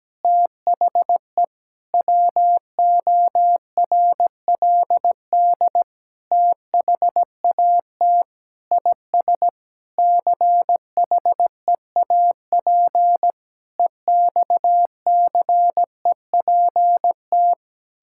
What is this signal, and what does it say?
Morse code, heard as a keyed tone, about 17 wpm, 700 Hz. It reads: THE WORLD THAT IS CHEAP EXCEPT
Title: Sentences from Top 1600 Words 17wpm